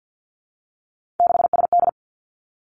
Morse code, heard as a keyed tone, about 50 wpm, 700 Hz. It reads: 6SD
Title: Three Challenging Letter-Number Combinations 50wpm